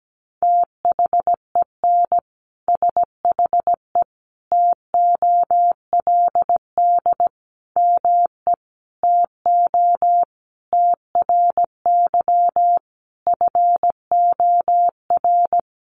Morse code, heard as a keyed tone, about 17 wpm, 700 Hz. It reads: THEN SHE TOLD ME TO TRY FOR
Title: Sentences from Top 400 Words 17wpm